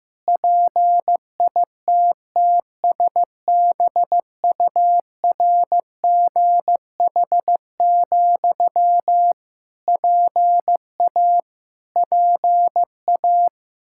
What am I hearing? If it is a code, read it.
Morse code, heard as a keyed tone, about 15 wpm, 700 Hz. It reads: PITTSBURGH, PA PA
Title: Practice QSO - Exchange 1 - Beginning - 15wpm